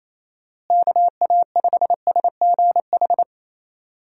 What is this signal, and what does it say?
Morse code, heard as a keyed tone, about 28 wpm, 700 Hz. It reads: KA5SGH
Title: USA Call Signs - 6 Characters - Repeated 2x - 28wpm